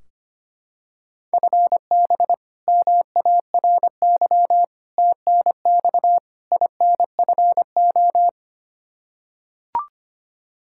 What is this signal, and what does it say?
Morse code, heard as a keyed tone, about 25 wpm, 700 Hz. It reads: FB MARY TNX INFO
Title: Practice QSO - Exchange 3 - More Info - 25wpm